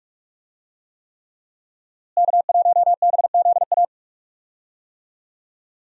Morse code, heard as a keyed tone, about 41 wpm, 700 Hz. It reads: K1BZA
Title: USA Call Signs - 5 Characters - Repeated 3x - 45wpm